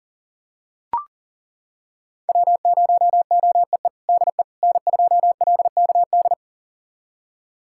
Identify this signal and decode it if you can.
Morse code, heard as a keyed tone, about 37 wpm, 700 Hz. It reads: W0OEE DE N2LKD
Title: USA Callsign DE Callsign - 5 Characters - 40wpm